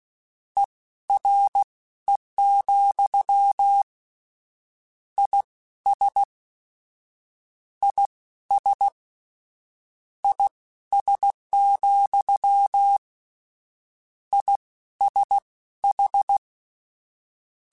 Morse code, heard as a keyed tone, about 16 wpm, spaced as for 12 wpm, 800 Hz. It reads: ERE, IS IS IS, ISH
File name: General / General-mp3s / Suffixes sp @15